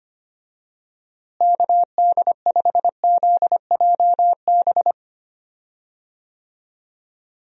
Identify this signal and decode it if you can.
Morse code, heard as a keyed tone, about 25 wpm, 700 Hz. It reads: KD5ZJB
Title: USA Call Signs - 6 Characters - Repeated 3x - 25wpm